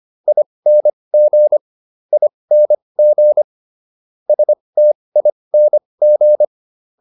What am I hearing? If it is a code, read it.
Morse code, heard as a keyed tone, about 25 wpm, spaced as for 20 wpm, 600 Hz. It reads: ING ING STING